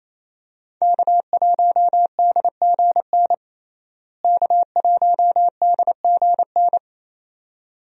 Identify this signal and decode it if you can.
Morse code, heard as a keyed tone, about 28 wpm, 700 Hz. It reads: K1DGN K1DGN
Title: USA Call Signs - 5 Characters - Repeated 2x - 28wpm